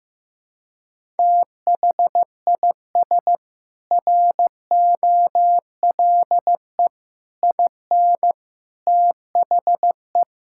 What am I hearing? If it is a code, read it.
Morse code, heard as a keyed tone, about 15 wpm, 700 Hz. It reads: THIS ROLE IN THE
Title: Sentences from Top 1000 Words - Review Part 6 - 15wpm